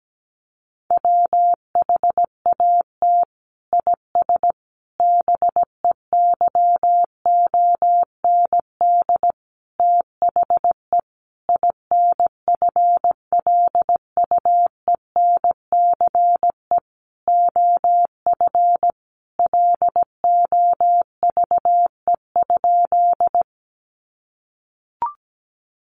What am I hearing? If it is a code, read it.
Morse code, heard as a keyed tone, about 17 wpm, 700 Hz. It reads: WHAT IS BEYOND THE INFLUENCE OF LOVE?
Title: Sentences from Top 1100 Words 17wpm